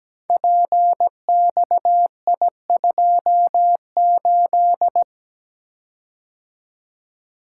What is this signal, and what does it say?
Morse code, heard as a keyed tone, about 17 wpm, 700 Hz. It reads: PXI28
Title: Code Groups - 5 Letter:Numbers - 17wpm